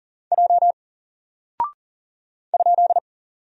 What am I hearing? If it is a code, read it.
Morse code, heard as a keyed tone, about 40 wpm, 700 Hz. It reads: J ?
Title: Single Character (Letter:Number:Punctuation:Prosign) 40wpm